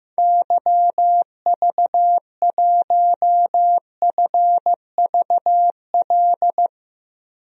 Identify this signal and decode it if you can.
Morse code, heard as a keyed tone, about 15 wpm, 700 Hz. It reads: YV1FVL
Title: DXCC-100 Call Signs - 6 Characters - Repeated 2x - 15wpm